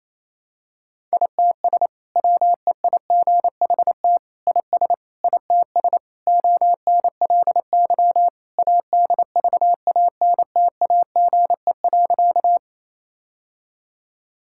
Morse code, heard as a keyed tone, about 28 wpm, 700 Hz. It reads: ITS WEIGHT IS ITS ONLY ADVANTAGE.
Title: Sentences from Top 1000 Words - Review Part 6 - 28wpm